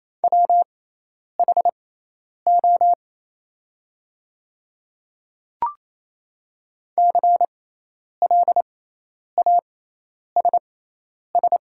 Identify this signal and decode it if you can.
Morse code, heard as a keyed tone, about 28 wpm, spaced as for 10 wpm, 700 Hz. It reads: WHO CLASS
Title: 6x Character Spacing - Top 500 Words - 28wpm